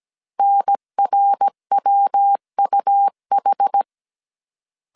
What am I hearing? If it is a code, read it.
Morse code, heard as a keyed tone, about 17 wpm, 800 Hz. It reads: NRWUH